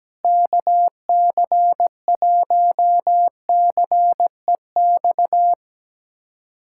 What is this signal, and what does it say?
Morse code, heard as a keyed tone, about 17 wpm, 700 Hz. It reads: KC1CEX
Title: USA Callsign DE Callsign - 6 Characters - 17wpm